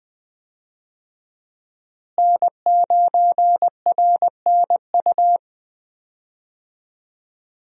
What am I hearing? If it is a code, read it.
Morse code, heard as a keyed tone, about 20 wpm, 700 Hz. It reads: N9RNU
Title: USA Call Signs - 5 Characters - Repeated 2x - 20wpm